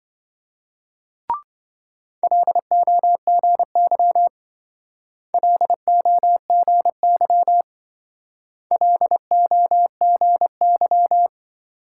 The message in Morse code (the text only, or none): LOGY LOGY LOGY